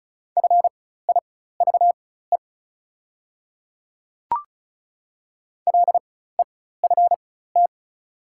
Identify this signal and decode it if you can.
Morse code, heard as a keyed tone, about 35 wpm, spaced as for 16 wpm, 690 Hz. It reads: FIVE LEFT
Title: 4x Character Spacing - Top 500 Words - 35wpm